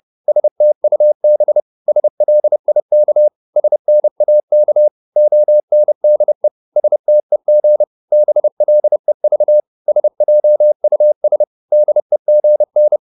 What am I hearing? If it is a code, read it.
Morse code, heard as a keyed tone, about 30 wpm, 600 Hz. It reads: STUB SLIK SNAK ONDE STEG BLEV SJUS DEGN
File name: Korte ord DK 30wpm